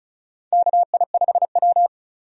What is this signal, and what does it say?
Morse code, heard as a keyed tone, about 34 wpm, 700 Hz. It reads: KI5W
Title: USA Call Signs - 4 Characters - 35wpm